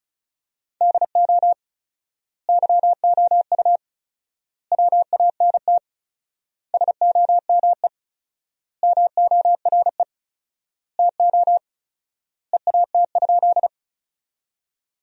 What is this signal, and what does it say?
Morse code, heard as a keyed tone, about 34 wpm, 700 Hz. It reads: DO YOU WANT SOME MORE TO EAT?
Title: 4x Word-Spacing - Sentences from Top 300 Words - 35wpm